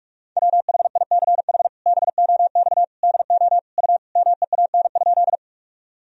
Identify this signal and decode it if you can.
Morse code, heard as a keyed tone, about 41 wpm, 700 Hz. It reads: WHICH BOX DO U MEAN?
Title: Sentences from Top 500 Words and Abbreviations 45wpm